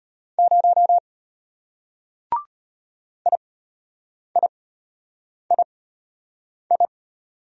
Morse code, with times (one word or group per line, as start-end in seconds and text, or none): 0.4-1.0 0
3.3-6.9 IIII